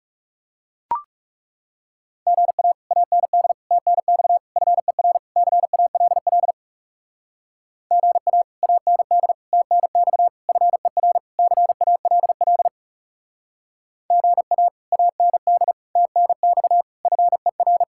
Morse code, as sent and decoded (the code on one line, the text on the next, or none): --. .- / .- -. -.. / - -. -..- / ..-. .-. / -.-. .- .-.. .-.. / --. .- / .- -. -.. / - -. -..- / ..-. .-. / -.-. .- .-.. .-.. / --. .- / .- -. -.. / - -. -..- / ..-. .-.
GA AND TNX FER CALL GA AND TNX FER CALL GA AND TNX FER